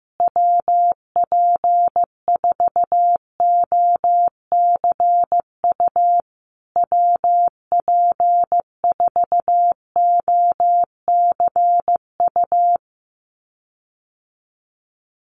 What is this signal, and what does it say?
Morse code, heard as a keyed tone, about 15 wpm, 700 Hz. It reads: WP4OCU WP4OCU
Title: USA Call Signs Repeated 3x 15wpm